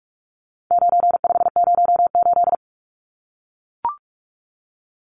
Morse code, heard as a keyed tone, about 44 wpm, 700 Hz. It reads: 9508
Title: Numbers - 4-Digit 45wpm